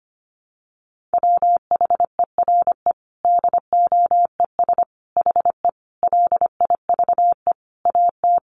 Morse code, heard as a keyed tone, about 25 wpm, 700 Hz. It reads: WHERE DOES HE LIVE AT
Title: Sentences from Top 800 Words 25wpm